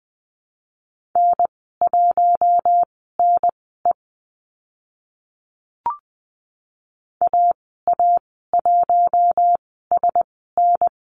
Morse code, heard as a keyed tone, about 20 wpm, spaced as for 15 wpm, 700 Hz. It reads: N1NE AA1SN
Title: 042 Lesson 8g - Number 1 - Call Signs 20wpm